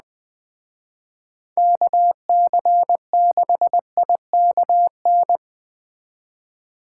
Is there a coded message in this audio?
Yes: Morse code from 1.6 to 5.4 s, heard as a keyed tone, about 20 wpm, 700 Hz. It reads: KC6IKN